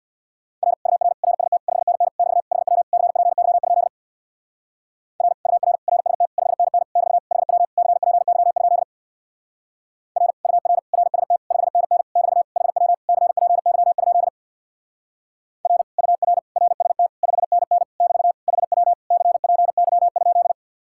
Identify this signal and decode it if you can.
Morse code, heard as a keyed tone, about 62 wpm, 700 Hz. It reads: R UR RST 5NN <BT> HW CPY? R UR RST 5NN <BT> HW CPY? R UR RST 5NN <BT> HW CPY? R UR RST 5NN <BT> HW CPY?
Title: Speed-Racing Common QSO Phrases Encore - 50wpm